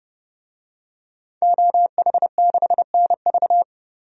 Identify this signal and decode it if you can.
Morse code, heard as a keyed tone, about 32 wpm, 700 Hz. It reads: OH6NV
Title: DXCC-100 Call Signs - 5 Characters - 30wpm